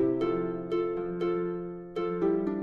There is no code.